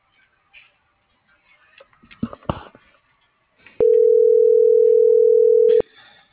Ringback tone (North America), 440 + 480 Hz.